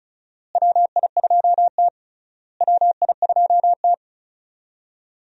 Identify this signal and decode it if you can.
Morse code, heard as a keyed tone, about 36 wpm, 700 Hz. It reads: WI2T WI2T
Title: USA Call Signs - 4 Characters - Repeated 3x - 35wpm